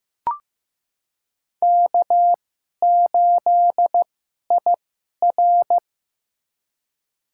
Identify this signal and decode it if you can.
Morse code, heard as a keyed tone, about 15 wpm, spaced as for 11 wpm, 700 Hz. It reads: K8IR
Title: USA Call Signs - 4 Characters - 2x Character Spacing - 15wpm